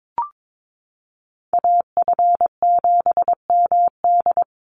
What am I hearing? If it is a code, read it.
Morse code, heard as a keyed tone, about 22 wpm, 700 Hz. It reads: AF7MD
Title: NA SOTA Chaser Callsigns 22wpm